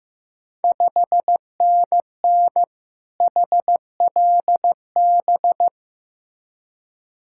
Morse code, heard as a keyed tone, about 15 wpm, 700 Hz. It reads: 5NN HLB
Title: Ontario QSO Party Exchange 15wpm